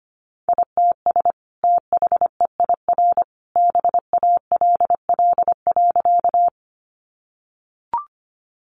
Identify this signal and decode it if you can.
Morse code, heard as a keyed tone, about 25 wpm, 700 Hz. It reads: ITS THEIR BALL.
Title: Sentences from Top 900 Words 25wpm